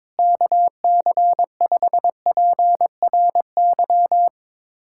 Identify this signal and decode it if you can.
Morse code, heard as a keyed tone, about 21 wpm, 700 Hz. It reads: KC5PRY
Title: USA Callsign DE Callsign - 6 Characters - 22wpm